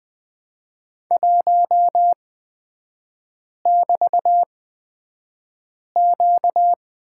Morse code, heard as a keyed tone, about 20 wpm, spaced as for 5 wpm, 690 Hz. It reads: 1<BT>Q
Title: 236 Lesson 38e - Letter q - Three Characters 20wpm